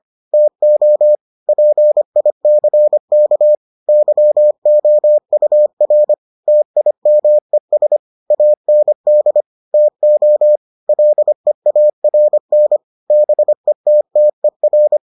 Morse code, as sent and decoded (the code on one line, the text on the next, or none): - --- / .--. .. -.-. -.- / -.-- --- ..- .-. / - .. -- . ... / .- -. -.. / - --- / .-.. .- .-. -. / -... - - . .-.
TO PICK YOUR TIMES AND TO LEARN BETTER